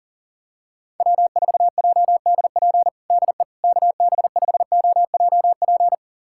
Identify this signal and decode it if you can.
Morse code, heard as a keyed tone, about 40 wpm, 700 Hz. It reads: W4JDP DE KB5OJP
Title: USA Callsign DE Callsign 40wpm